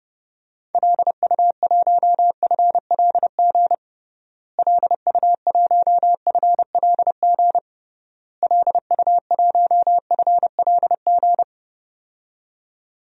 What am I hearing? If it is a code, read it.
Morse code, heard as a keyed tone, about 30 wpm, 700 Hz. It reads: LU1FLG LU1FLG LU1FLG
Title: DXCC-100 Call Signs - 6 Characters - Repeated 3x - 30wpm